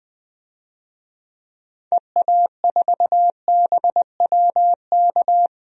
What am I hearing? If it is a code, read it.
Morse code, heard as a keyed tone, about 20 wpm, 700 Hz. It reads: EA4BWK
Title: DXCC-100 Call Signs - 6 Characters - Repeated 2x - 20wpm